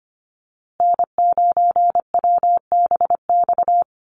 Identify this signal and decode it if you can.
Morse code, heard as a keyed tone, about 25 wpm, 700 Hz. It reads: N9WBX
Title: USA Call Signs Repeated 2x 25wpm